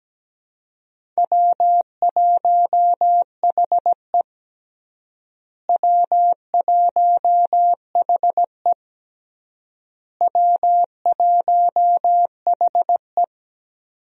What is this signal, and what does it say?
Morse code, heard as a keyed tone, about 17 wpm, 700 Hz. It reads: W1HE W1HE W1HE
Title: USA Call Signs - 4 Characters - Repeated 3x - 17wpm